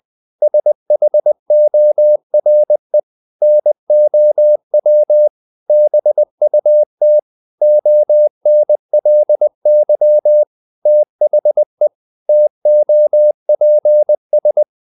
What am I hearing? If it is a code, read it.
Morse code, heard as a keyed tone, about 20 wpm, 600 Hz. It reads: SHORE NOW BUT ONLY THE TOPS
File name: the old man and the sea / the old man and the sea part 1 20wpm